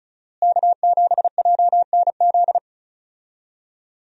Morse code, heard as a keyed tone, about 36 wpm, 700 Hz. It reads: K7JNZ
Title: USA Call Signs - 5 Characters - Repeated 3x - 35wpm